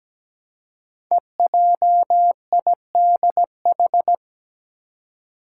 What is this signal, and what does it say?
Morse code, heard as a keyed tone, about 17 wpm, 700 Hz. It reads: EJIDH
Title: Code Groups - 5 Letters - 17wpm